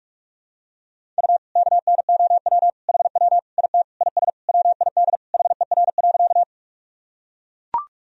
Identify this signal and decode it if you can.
Morse code, heard as a keyed tone, about 41 wpm, 700 Hz. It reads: U KNOW HW IT IS WID HER.
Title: Sentences from Top 100 Words and Abbreviations 45wpm